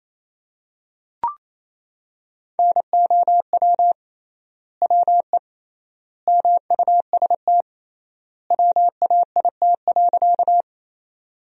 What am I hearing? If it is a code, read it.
Morse code, heard as a keyed tone, about 28 wpm, 700 Hz. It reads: NOW WE MUST WAIT.
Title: 3x Word-Spacing - Sentences from Top 500 Words - 28wpm